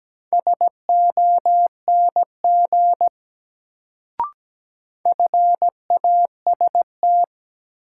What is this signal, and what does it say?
Morse code, heard as a keyed tone, about 17 wpm, 700 Hz. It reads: SONG FAST
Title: Sentences from Top 400 Words - New Word Drill - 17wpm